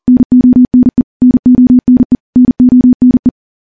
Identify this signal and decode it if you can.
Morse code, heard as a keyed tone, about 40 wpm, 250 Hz. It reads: NONE NONE NONE
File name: morse / morse-training-40WPM